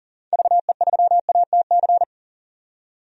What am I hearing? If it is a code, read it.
Morse code, heard as a keyed tone, about 40 wpm, 700 Hz. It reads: VE3ATC
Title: DXCC-100 Call Signs - 6 Characters - 40wpm